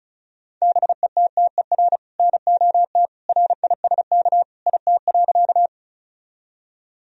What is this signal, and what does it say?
Morse code, heard as a keyed tone, about 36 wpm, 700 Hz. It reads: BETTER NOT RISK IT.
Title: Sentences from Top 1000 Words - Review Part 4 - 35wpm